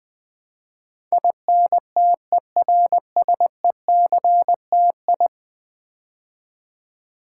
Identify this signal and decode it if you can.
Morse code, heard as a keyed tone, about 20 wpm, 700 Hz. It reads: INTERSECTI
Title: Build A Word - 5 Letters At A Time - 12-Letter Words - 20wpm